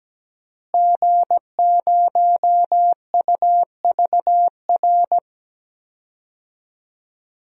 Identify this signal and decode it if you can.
Morse code, heard as a keyed tone, about 17 wpm, 700 Hz. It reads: G0UVR
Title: DXCC-100 Call Signs - 5 Characters - Repeated 3x - 17wpm